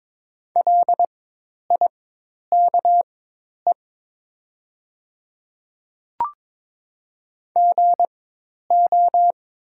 Morse code, heard as a keyed tone, about 22 wpm, spaced as for 10 wpm, 700 Hz. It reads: LIKE GO
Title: 4x Character Spacing - Top 100 Words - 22wpm